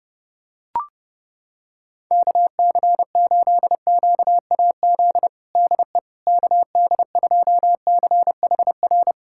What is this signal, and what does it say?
Morse code, heard as a keyed tone, about 32 wpm, 700 Hz. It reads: KC8QAZ DE KD2CHR
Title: USA Callsign DE Callsign - 6 Characters - 30wpm